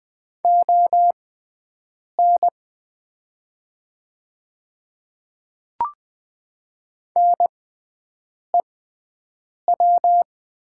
Morse code, heard as a keyed tone, about 20 wpm, spaced as for 7 wpm, 700 Hz. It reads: ON NEW
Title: 6x Character Spacing - Top 200 Words - 20wpm